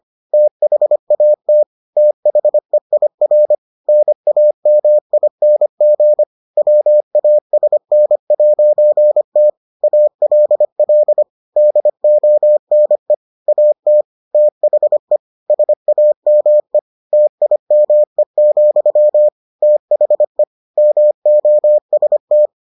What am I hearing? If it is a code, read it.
Morse code, heard as a keyed tone, about 25 wpm, 600 Hz. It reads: THAT THEIR NAMING WASN'T ALL DONE AT THE SAME TIME, THE MOST